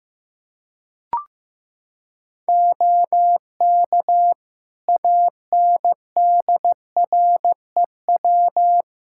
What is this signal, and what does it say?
Morse code, heard as a keyed tone, about 15 wpm, 700 Hz. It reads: OK ANDREW
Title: Practice QSO - Exchange 3 - More Info - 15wpm